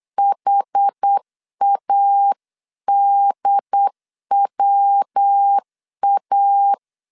Morse code, heard as a keyed tone, about 8 wpm, 800 Hz. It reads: HADWA